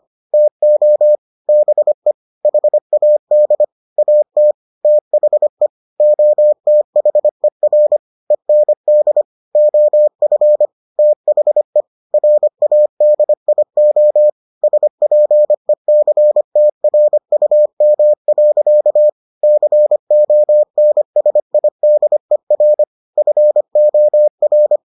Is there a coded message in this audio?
Yes: Morse code from 0.3 to 24.8 s, heard as a keyed tone, about 25 wpm, 600 Hz. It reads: TO BE HAD AT THE OTHER END OF THE RADIO SPECTRUM. CONSIDER FOR